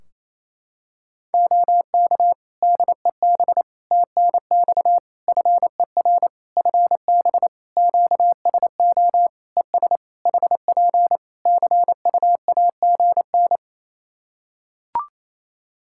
Morse code, heard as a keyed tone, about 28 wpm, 700 Hz. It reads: OK DEB TNX FER FB QSO ES HP CUAGN
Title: Practice QSO - Exchange 4 - Ending - 28wpm